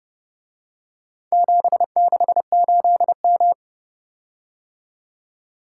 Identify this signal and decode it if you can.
Morse code, heard as a keyed tone, about 30 wpm, 700 Hz. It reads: 768M